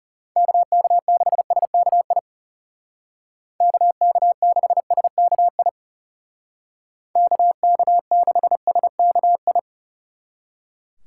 Morse code, heard as a keyed tone, about 34 wpm, 700 Hz. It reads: KK6SKI KK6SKI KK6SKI
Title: Speed Racing - USA Call Signs 30wpm